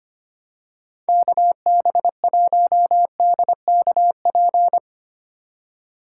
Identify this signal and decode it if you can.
Morse code, heard as a keyed tone, about 25 wpm, 700 Hz. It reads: KB1DKP
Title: USA Call Signs - 6 Characters - 4x Character Spacing - 25wpm